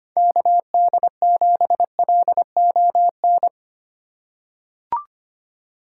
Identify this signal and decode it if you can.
Morse code, heard as a keyed tone, about 25 wpm, 700 Hz. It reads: KD7LON
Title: USA Call Signs - 6 Characters - 4x Character Spacing - 25wpm